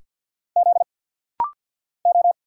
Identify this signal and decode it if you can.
Morse code, heard as a keyed tone, about 48 wpm, 690 Hz. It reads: Z O
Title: Single Letters - Rapid-Fire 50wpm